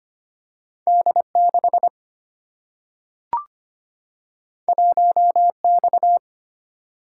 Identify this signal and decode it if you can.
Morse code, heard as a keyed tone, about 25 wpm, 700 Hz. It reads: D6 1X
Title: Single-Letter Single-Number 25wpm